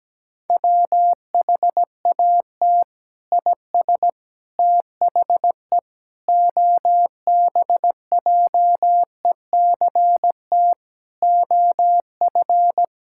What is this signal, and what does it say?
Morse code, heard as a keyed tone, about 17 wpm, 700 Hz. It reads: WHAT IS THE OBJECT OF
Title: Sentences from Top 1000 Words - Review Part 5 - 17wpm